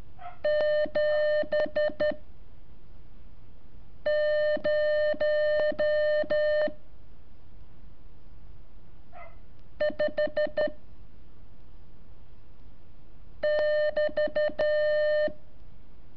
Morse code, heard as a keyed tone, about 12 wpm, 600 Hz. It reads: 7 0 5 <BT>
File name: curso / ejercicio 22